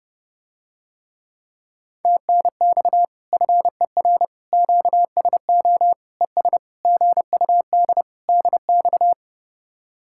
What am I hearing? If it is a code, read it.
Morse code, heard as a keyed tone, about 30 wpm, 700 Hz. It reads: TNX FER QSO ES GUD DX